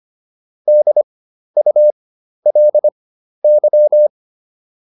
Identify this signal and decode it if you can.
Morse code, heard as a keyed tone, about 25 wpm, spaced as for 12 wpm, 600 Hz. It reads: DULY